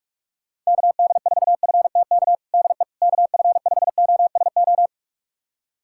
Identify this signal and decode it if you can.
Morse code, heard as a keyed tone, about 46 wpm, 700 Hz. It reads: KD4FTK DE KF5OSO
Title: USA Callsign DE Callsign 45wpm